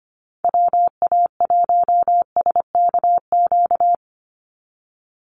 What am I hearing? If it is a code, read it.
Morse code, heard as a keyed tone, about 25 wpm, 700 Hz. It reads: WA1SKQ
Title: Call Signs - 25wpm